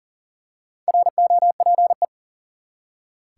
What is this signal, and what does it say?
Morse code, heard as a keyed tone, about 38 wpm, 700 Hz. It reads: ROPE
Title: Four Letter Words - 40wpm